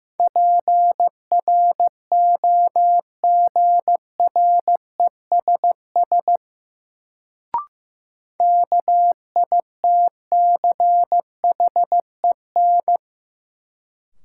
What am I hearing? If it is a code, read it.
Morse code, heard as a keyed tone, about 15 wpm, 700 Hz. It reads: PROGRESS KITCHEN
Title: Sentences from Top 1400 Words - New Word Drill - 15wpm